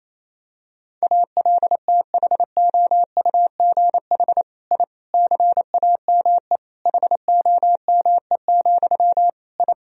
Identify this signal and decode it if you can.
Morse code, heard as a keyed tone, about 27 wpm, 700 Hz. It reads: ALTHOUGH I CAME HOME, I
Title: Sentences from Top 1000 Words - Review Part 3 - 28wpm